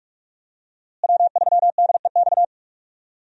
Morse code, heard as a keyed tone, about 46 wpm, 700 Hz. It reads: W3DEX
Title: USA Call Signs - 5 Characters - Repeated 2x - 45wpm